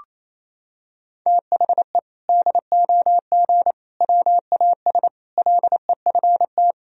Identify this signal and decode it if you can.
Morse code, heard as a keyed tone, about 28 wpm, 700 Hz. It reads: THE DOG WAS LEFT
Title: Sentences from Top 1000 Words - Review Part 2 - 28wpm